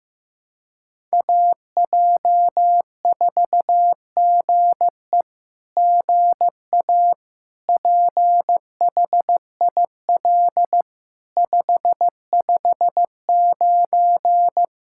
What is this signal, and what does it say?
Morse code, heard as a keyed tone, about 15 wpm, 700 Hz. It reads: AJ4GE GA PHIL 559
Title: SOTA QSO Practice 15wpm